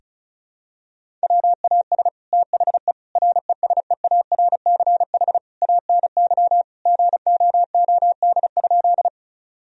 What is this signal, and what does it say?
Morse code, heard as a keyed tone, about 34 wpm, 700 Hz. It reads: WAS THE RESEARCH ANY GOOD?
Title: Sentences from Top 1000 Words - Review Part 4 - 35wpm